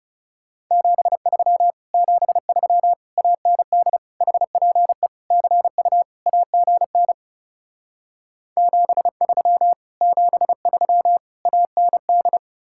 Morse code, heard as a keyed tone, about 37 wpm, 700 Hz. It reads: 73 73 AND HPE CU AGN 73 73 AND
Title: Speed-Racing Common QSO Phrases - 30wpm